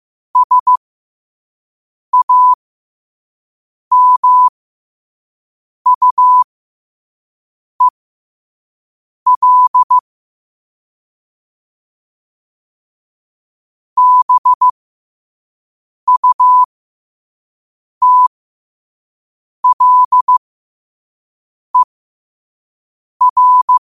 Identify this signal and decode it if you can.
Morse code, heard as a keyed tone, about 15 wpm, spaced as for 5 wpm, 1000 Hz. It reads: SAMUEL BUTLER
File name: qotdfiles / 5wpm